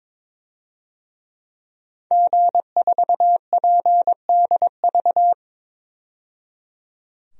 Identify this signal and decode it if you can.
Morse code, heard as a keyed tone, about 22 wpm, 700 Hz. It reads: G4PDV
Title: DXCC-100 Call Signs - 5 Characters - 22wpm